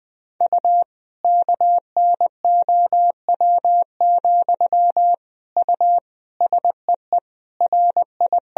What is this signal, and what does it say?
Morse code, heard as a keyed tone, about 19 wpm, 700 Hz. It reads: U KNOW, U SEE RI
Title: Sentences from Top 200 Words and Abbreviations 20wpm